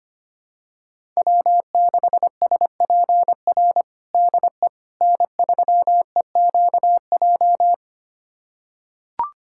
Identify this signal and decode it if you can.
Morse code, heard as a keyed tone, about 25 wpm, 700 Hz. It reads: W6SPR DE N3EQJ
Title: USA Callsign DE Callsign - 5 Characters - 25wpm